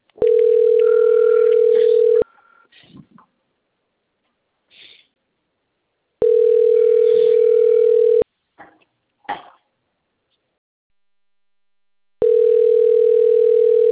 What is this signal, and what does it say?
Ringback tone (North America), 440 + 480 Hz.
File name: out- - - - - .20558